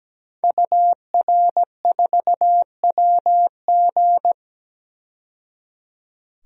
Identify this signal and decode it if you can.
Morse code, heard as a keyed tone, about 17 wpm, 700 Hz. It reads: UR4WG
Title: DXCC-100 Call Signs - 5 Characters - 17wpm